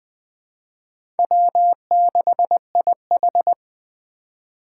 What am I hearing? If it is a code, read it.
Morse code, heard as a keyed tone, about 21 wpm, 700 Hz. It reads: W6IH
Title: USA Call Signs - 4 Characters - 8x Character Spacing - 20wpm